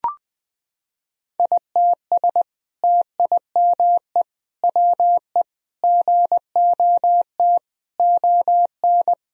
Morse code, heard as a keyed tone, about 19 wpm, 700 Hz. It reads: ITS TIME WE GOT ON
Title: Sentences from Top 1000 Words - Review Part 4 - 20wpm